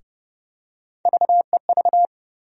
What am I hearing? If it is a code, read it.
Morse code, heard as a keyed tone, about 31 wpm, 700 Hz. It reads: VEV
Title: USA Call Sign Postfixes - 3 Letters 30wpm